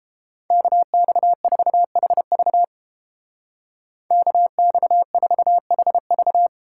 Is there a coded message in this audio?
Yes: Morse code from 0.5 to 6.5 s, heard as a keyed tone, about 30 wpm, 700 Hz. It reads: KX4HV KX4HV